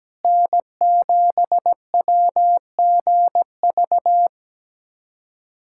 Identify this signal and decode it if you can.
Morse code, heard as a keyed tone, about 17 wpm, 700 Hz. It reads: N7WGV
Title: USA Callsign DE Callsign - 5 Characters - 17wpm